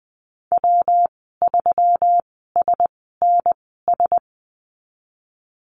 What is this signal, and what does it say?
Morse code, heard as a keyed tone, about 20 wpm, spaced as for 15 wpm, 700 Hz. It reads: W3SNS
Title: 173 Lesson 28g - Number 9 - Call Signs 20wpm